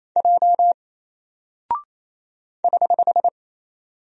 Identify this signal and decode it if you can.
Morse code, heard as a keyed tone, about 28 wpm, 700 Hz. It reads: J <HH>
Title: Single Character (Letter:Number:Punctuation:Prosign) Alt 28wpm